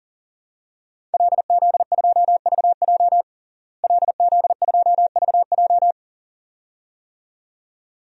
Morse code, heard as a keyed tone, about 40 wpm, 700 Hz. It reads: LZ2VJ LZ2VJ
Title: DXCC-100 Call Signs - 5 Characters - Repeated 2x - 40wpm